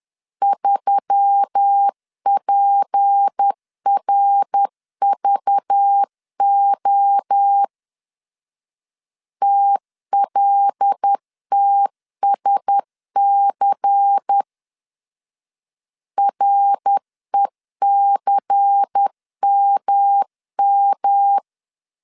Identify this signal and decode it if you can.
Morse code, heard as a keyed tone, about 10 wpm, 800 Hz. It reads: 3PRVO TLTSC RECMM
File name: CW / DET / 1 6 1500C S C 10wpm 4252